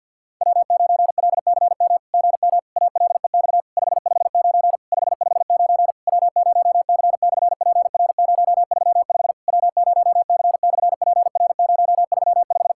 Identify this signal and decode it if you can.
Morse code, heard as a keyed tone, about 48 wpm, 690 Hz. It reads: W9LCM GM ALEX 559 559 W0C/PR035 W0C/PR035
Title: SOTA QSO Practice 50wpm